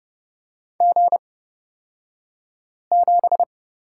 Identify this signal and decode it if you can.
Morse code, heard as a keyed tone, about 31 wpm, spaced as for 5 wpm, 700 Hz. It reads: G7
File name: Lesson 36d - BT - Two Characters 30wpm